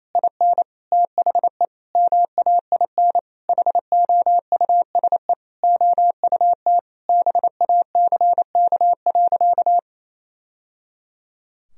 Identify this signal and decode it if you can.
Morse code, heard as a keyed tone, about 27 wpm, 700 Hz. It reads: IN THE MAIN HOUSE OUT BACK.
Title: Sentences from Top 1000 Words - Review Part 2 - 28wpm